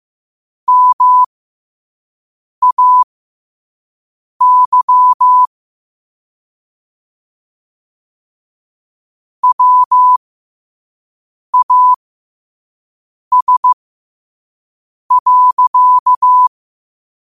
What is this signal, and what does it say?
Morse code, heard as a keyed tone, about 15 wpm, spaced as for 5 wpm, 1000 Hz. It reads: MAY WAS.
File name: qotdfiles / 5wpm